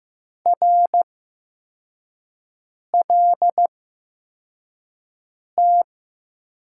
Morse code, heard as a keyed tone, about 15 wpm, spaced as for 4 wpm, 700 Hz. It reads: RLT
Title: USA Call Signs - 8x Character Spacing - 15wpm